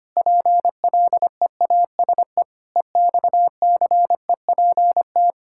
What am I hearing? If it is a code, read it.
Morse code, heard as a keyed tone, about 25 wpm, 700 Hz. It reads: PLEASE EXCEPT
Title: Sentences from Top 1600 Words 25wpm